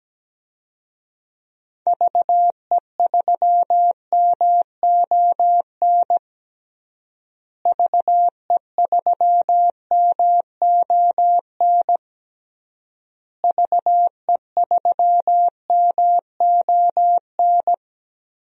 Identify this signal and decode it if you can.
Morse code, heard as a keyed tone, about 17 wpm, 700 Hz. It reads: VE3MON VE3MON VE3MON
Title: DXCC-100 Call Signs - 6 Characters - Repeated 3x - 17wpm